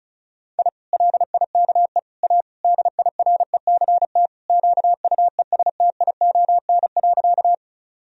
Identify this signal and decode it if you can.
Morse code, heard as a keyed tone, about 34 wpm, 700 Hz. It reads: I LIKE A DIRECT QUESTION.
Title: Sentences from Top 1000 Words - Review Part 2 - 35wpm